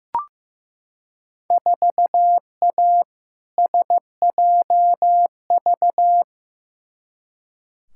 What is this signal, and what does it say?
Morse code, heard as a keyed tone, about 15 wpm, 700 Hz. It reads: 4A SJV
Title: Field Day Exchanges Encore - 15wpm